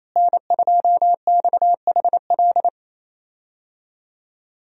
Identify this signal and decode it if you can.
Morse code, heard as a keyed tone, about 28 wpm, 700 Hz. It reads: N2XHL
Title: USA Call Signs - 5 Characters - Repeated 2x - 28wpm